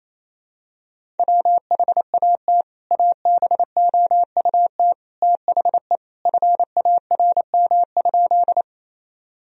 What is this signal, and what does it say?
Morse code, heard as a keyed tone, about 28 wpm, 700 Hz. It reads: WHAT ABOUT THE FARM?
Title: Sentences from Top 1000 Words - Review Part 2 - 28wpm